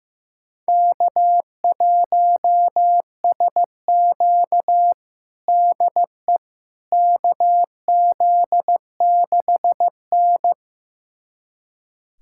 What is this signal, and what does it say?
Morse code, heard as a keyed tone, about 16 wpm, 700 Hz. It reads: K1SQ DE KZ6N
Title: USA Callsign DE Callsign - 4 Characters - 15wpm